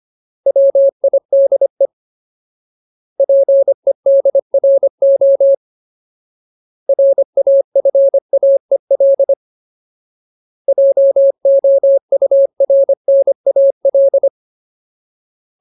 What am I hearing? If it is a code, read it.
Morse code, heard as a keyed tone, about 25 wpm, 550 Hz. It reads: WIDE PEDRO RAFAEL JOURNAL
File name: Ultimate Word List / 25WPM Ultimate